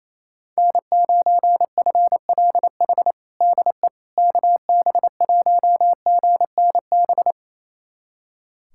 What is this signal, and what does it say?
Morse code, heard as a keyed tone, about 28 wpm, 700 Hz. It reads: N9FLH DE KB1GNB
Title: USA Callsign DE Callsign 28wpm